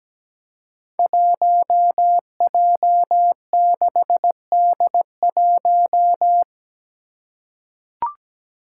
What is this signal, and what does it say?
Morse code, heard as a keyed tone, about 17 wpm, 700 Hz. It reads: 1J6D1
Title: Five Challenging Letter-Number Combinations 17wpm